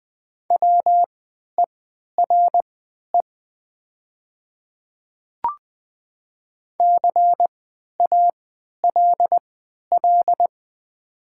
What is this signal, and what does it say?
Morse code, heard as a keyed tone, about 20 wpm, spaced as for 11 wpm, 700 Hz. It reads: WERE CALL
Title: 3x Character Spacing - Top 100 Words - 20wpm